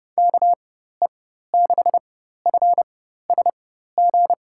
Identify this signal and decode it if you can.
Morse code, heard as a keyed tone, about 30 wpm, spaced as for 14 wpm, 700 Hz. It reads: KE6FSG
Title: USA Call Signs - 4x Character Spacing - 30wpm